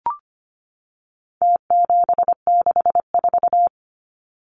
Morse code, heard as a keyed tone, about 25 wpm, 700 Hz. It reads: T764
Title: Numbers - Contest Style - 4-Digit 25wpm